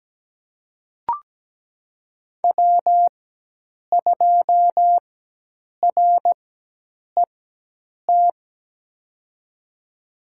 Morse code, heard as a keyed tone, about 17 wpm, spaced as for 8 wpm, 700 Hz. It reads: W2RET
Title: USA Call Signs - 5 Characters - 4x Character Spacing - 17wpm